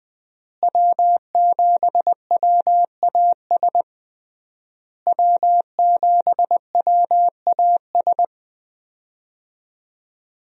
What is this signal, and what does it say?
Morse code, heard as a keyed tone, about 20 wpm, 700 Hz. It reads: W7WAS W7WAS
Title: USA Call Signs - 5 Characters - Repeated 3x - 20wpm